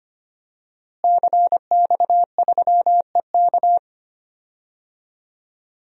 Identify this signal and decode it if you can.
Morse code, heard as a keyed tone, about 25 wpm, 700 Hz. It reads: CX3EK
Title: DXCC-100 Call Signs - 5 Characters - Repeated 2x - 25wpm